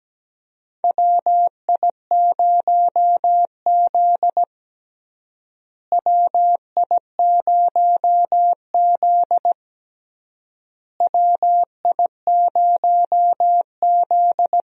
Morse code, heard as a keyed tone, about 17 wpm, 700 Hz. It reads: WI0Z WI0Z WI0Z
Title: USA Call Signs - 4 Characters - Repeated 3x - 17wpm